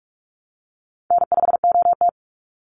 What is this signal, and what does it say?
Morse code, heard as a keyed tone, about 44 wpm, 700 Hz. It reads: N5OT
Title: Call Signs Repeated 3x - 45wpm